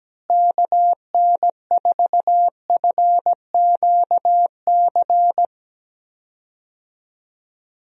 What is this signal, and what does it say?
Morse code, heard as a keyed tone, about 17 wpm, 700 Hz. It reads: KN4FQC
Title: USA Call Signs - 6 Characters - Repeated 2x - 17wpm